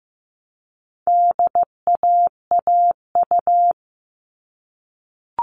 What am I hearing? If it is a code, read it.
Morse code, heard as a keyed tone, about 15 wpm, 700 Hz. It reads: DAAU
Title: All Cut Numbers - 4-Digit 15wpm